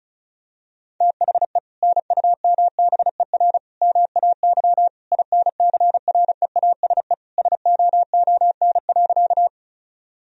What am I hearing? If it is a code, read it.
Morse code, heard as a keyed tone, about 34 wpm, 700 Hz. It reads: THE NUMBER MAY INCREASE SOON.
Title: Sentences from Top 1000 Words - Review Part 4 - 35wpm